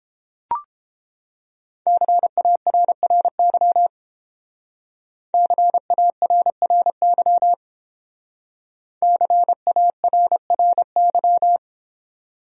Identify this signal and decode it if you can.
Morse code, heard as a keyed tone, about 32 wpm, 700 Hz. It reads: CARRY CARRY CARRY